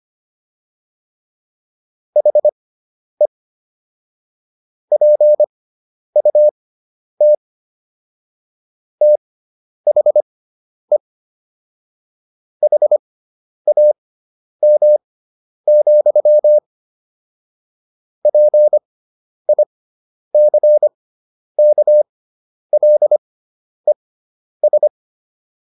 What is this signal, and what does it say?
Morse code, heard as a keyed tone, about 25 wpm, spaced as for 10 wpm, 600 Hz. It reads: HE PUT THE HAM, PICKLES